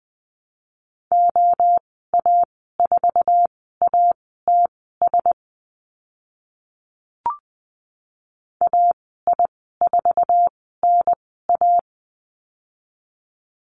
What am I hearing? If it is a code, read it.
Morse code, heard as a keyed tone, about 20 wpm, spaced as for 14 wpm, 700 Hz. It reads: OA4ATS AI4NA
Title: 076 Lesson 13g - Letter L - Call Signs 20wpm